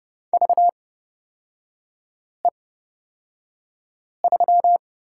Morse code, heard as a keyed tone, about 30 wpm, spaced as for 5 wpm, 700 Hz. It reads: VE3
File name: Lesson 36e - BT - Three Characters 30wpm